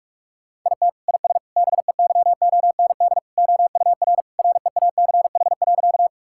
Morse code, heard as a keyed tone, about 46 wpm, 700 Hz. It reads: IT IS BEYOND OUR REACH.
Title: Sentences from Top 1000 Words - Review Part 6 - 45wpm